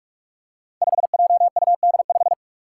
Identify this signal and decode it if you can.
Morse code, heard as a keyed tone, about 41 wpm, 700 Hz. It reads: 5JUD5
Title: Five Challenging Letter-Number Combinations 45wpm